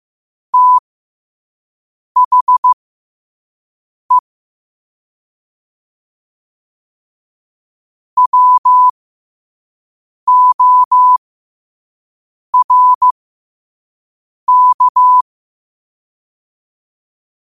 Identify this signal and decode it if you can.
Morse code, heard as a keyed tone, about 15 wpm, spaced as for 5 wpm, 1000 Hz. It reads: THE WORK
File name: qotdfiles / 5wpm